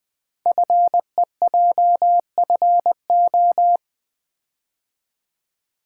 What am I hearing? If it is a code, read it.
Morse code, heard as a keyed tone, about 20 wpm, 700 Hz. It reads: FEJFO
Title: Code Groups - 5 Letters - 20wpm